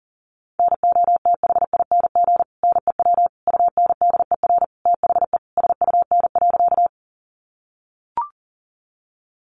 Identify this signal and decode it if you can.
Morse code, heard as a keyed tone, about 40 wpm, 700 Hz. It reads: NOTHING NEW UNDER THE SUN.
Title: Sentences from Top 500 Words 40wpm